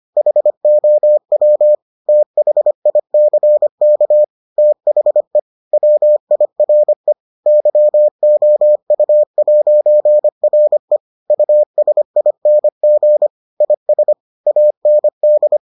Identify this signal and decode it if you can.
Morse code, heard as a keyed tone, about 25 wpm, 600 Hz. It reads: HOW THICK THE WIRE YOU'RE USING IS AND